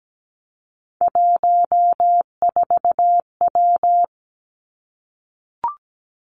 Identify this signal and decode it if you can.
Morse code, heard as a keyed tone, about 17 wpm, 700 Hz. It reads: 14W
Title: Three Challenging Letter-Number Combinations 17wpm